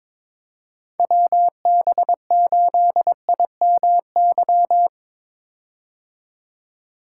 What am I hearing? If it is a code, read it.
Morse code, heard as a keyed tone, about 22 wpm, 700 Hz. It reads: WB8IMY
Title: K1USN SST QSO Practice - 22wpm